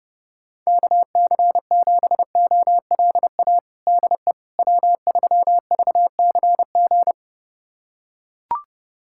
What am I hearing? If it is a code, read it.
Morse code, heard as a keyed tone, about 30 wpm, 700 Hz. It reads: KC7OLA DE W3VCG
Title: USA Callsign DE Callsign 30wpm